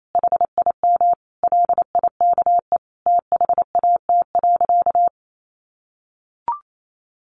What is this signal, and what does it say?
Morse code, heard as a keyed tone, about 28 wpm, 700 Hz. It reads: HIM LIKE THAT.
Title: Sentences from Top 1300 Words 28wpm